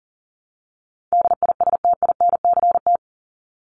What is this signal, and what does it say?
Morse code, heard as a keyed tone, about 40 wpm, 700 Hz. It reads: DISTINCT
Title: Top 1501-1750 Words 40wpm